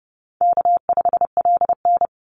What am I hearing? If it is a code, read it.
Morse code, heard as a keyed tone, about 30 wpm, 700 Hz. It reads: K5LN
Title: Easy Call Signs - 30wpm